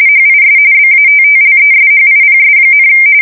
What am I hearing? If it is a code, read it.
RTTY, about 45.45 baud. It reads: HIS BATTLESHIPS,
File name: fsk170 / battleshipnj